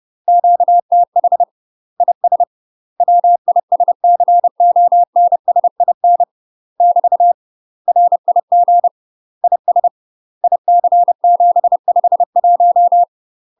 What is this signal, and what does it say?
Morse code, heard as a keyed tone, about 30 wpm, 700 Hz. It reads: QTH IS WISCONSIN <BT> RIG IS IC751
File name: UK MASTER QSOs TRAINING / 30WPM / UK MASTER QSOs 0004